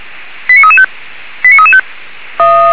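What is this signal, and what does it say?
Five-tone selective call (ZVEI), heard as quick runs of five tones, tones of about 70 ms. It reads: 89335 89335
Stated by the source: Five-tone ZVEI selective call 89315, received by SDR, fire brigade Orlamünde.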